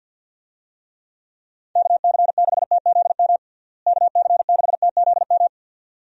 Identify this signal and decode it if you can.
Morse code, heard as a keyed tone, about 52 wpm, 700 Hz. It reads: KC6TZM KC6TZM
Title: USA Call Signs - 6 Characters - Repeated 2x - 50wpm